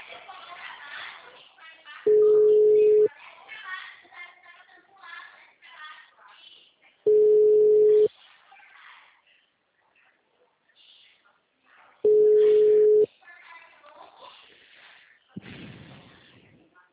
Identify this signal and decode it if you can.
Ringback tone (Europe, 425 Hz), with speech.